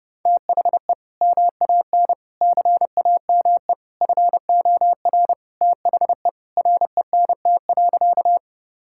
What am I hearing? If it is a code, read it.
Morse code, heard as a keyed tone, about 32 wpm, 700 Hz. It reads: THE MAN CAME FOR THE RENT.
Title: Sentences from Top 1000 Words - Review Part 5 - 30wpm